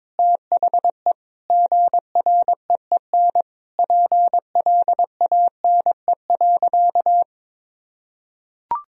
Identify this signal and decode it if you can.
Morse code, heard as a keyed tone, about 22 wpm, 700 Hz. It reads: THE GREEN PLANE.
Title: Sentences from Top 1000 Words - Review Part 3 - 22wpm